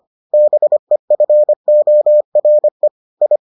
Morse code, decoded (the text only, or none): BEFORE I